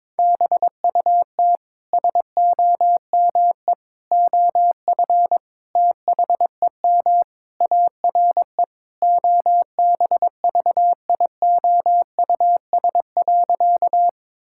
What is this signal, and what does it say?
Morse code, heard as a keyed tone, about 22 wpm, 700 Hz. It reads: BUT SOME OF THEM ARE OBVIOUS.
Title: Sentences from Top 1600 Words 22wpm